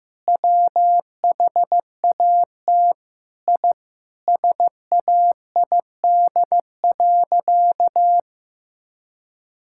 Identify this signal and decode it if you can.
Morse code, heard as a keyed tone, about 16 wpm, 700 Hz. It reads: WHAT I SAID.
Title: Sentences from Top 1000 Words - Review Part 6 - 15wpm